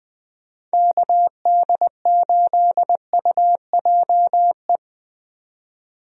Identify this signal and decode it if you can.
Morse code, heard as a keyed tone, about 20 wpm, 700 Hz. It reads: KD8UJE
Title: USA Call Signs - 6 Characters - 20wpm